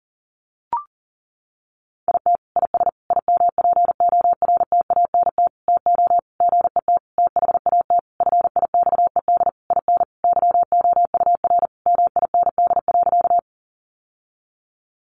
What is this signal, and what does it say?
Morse code, heard as a keyed tone, about 37 wpm, 700 Hz. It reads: IT IS IMPORTANT TO GET THAT FIXED IN YOUR MIND.
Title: Sentences from Top 1100 Words 40wpm